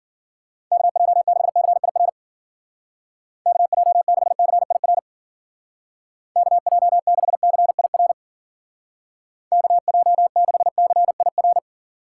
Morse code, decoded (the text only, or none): KJ6CIR KJ6CIR KJ6CIR KJ6CIR